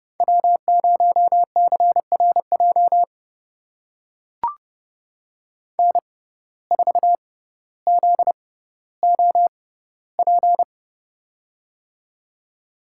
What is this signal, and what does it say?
Morse code, heard as a keyed tone, about 30 wpm, spaced as for 15 wpm, 700 Hz. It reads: W0CRJ N4ZOP
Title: USA Call Signs - 5 Characters - 6x Character Spacing - 30wpm